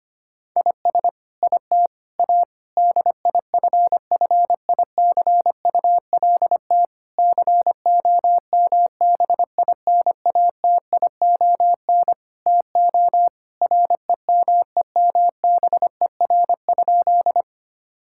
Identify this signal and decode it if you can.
Morse code, heard as a keyed tone, about 25 wpm, 700 Hz. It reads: IS IT A DIFFICULT COMBINATION TO REMEMBER?
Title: Sentences from Top 1000 Words - Review Part 6 - 25wpm